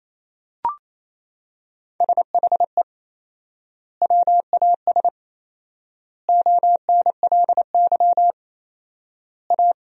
Morse code, heard as a keyed tone, about 27 wpm, 700 Hz. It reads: SHE WAS ONLY A
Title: 4x Word-Spacing - Sentences from Top 400 Words - 28wpm